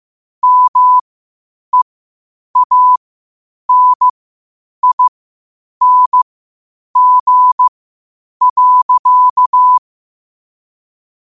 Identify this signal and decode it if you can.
Morse code, heard as a keyed tone, about 15 wpm, spaced as for 8 wpm, 1000 Hz. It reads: MEANING.
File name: qotdfiles / 8wpm